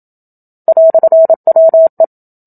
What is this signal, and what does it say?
Morse code, heard as a keyed tone, about 27 wpm, 650 Hz. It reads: "WE
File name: Book Two / 01-Chapter 1 - 27 WPM